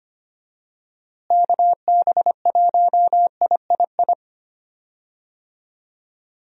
Morse code, heard as a keyed tone, about 25 wpm, 700 Hz. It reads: KB1III
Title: USA Call Signs - 6 Characters - Repeated 3x - 25wpm